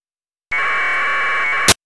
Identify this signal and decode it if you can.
Packet radio (AX.25, as in APRS).
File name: amsat / AFSK